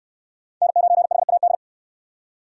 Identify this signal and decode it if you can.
Morse code, heard as a keyed tone, about 65 wpm, 700 Hz. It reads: N9HAD